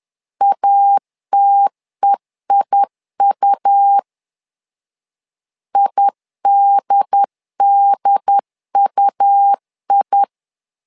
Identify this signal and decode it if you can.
Morse code, heard as a keyed tone, about 10 wpm, 800 Hz. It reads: ATEIU IDDUI
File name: LEZIONI / 2 UHDN / 2 UHD 10wpm 1826